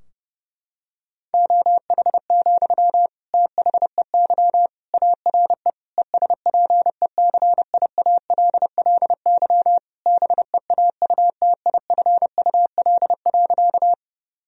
Morse code, heard as a keyed tone, about 30 wpm, 700 Hz. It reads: OH, THEY ARE ESPECIALLY BEAUTIFUL.
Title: Sentences from Top 1000 Words - Review Part 4 - 30wpm